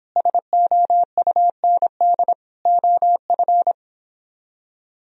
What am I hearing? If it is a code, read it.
Morse code, heard as a keyed tone, about 26 wpm, 700 Hz. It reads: SOUND OF